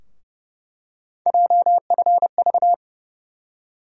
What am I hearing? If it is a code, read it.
Morse code, heard as a keyed tone, about 32 wpm, 700 Hz. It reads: JFV
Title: USA Call Sign Postfixes - 3 Letters 30wpm